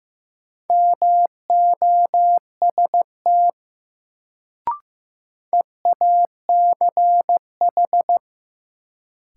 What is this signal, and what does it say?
Morse code, heard as a keyed tone, about 15 wpm, 700 Hz. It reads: MOST EACH
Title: Sets of 2 Words - New Word Drill - 15wpm